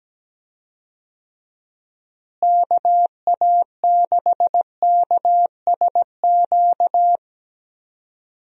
Morse code, heard as a keyed tone, about 17 wpm, 700 Hz. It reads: KA6KSQ
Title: USA Call Signs - 6 Characters - Repeated 3x - 17wpm